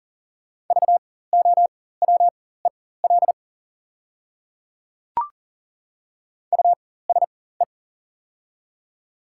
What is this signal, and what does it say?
Morse code, heard as a keyed tone, about 40 wpm, spaced as for 19 wpm, 690 Hz. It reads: VOWEL USE
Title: 4x Character Spacing - Top 500 Words - 40wpm